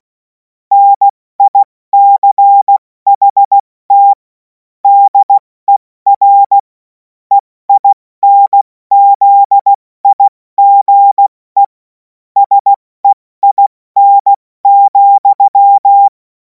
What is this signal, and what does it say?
Morse code, heard as a keyed tone, about 16 wpm, 800 Hz. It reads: NICHT DER EINZIGE SEIN,
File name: Morsekurs / Witze / Witz 07. Juli 2026 70 F80